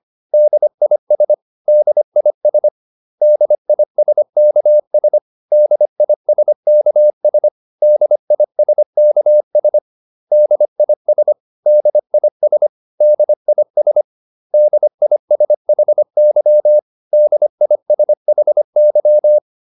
Morse code, heard as a keyed tone, about 25 wpm, 600 Hz. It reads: DIS DIS DISKS DISKS DISKS DIS DIS DIS DISHY DISHY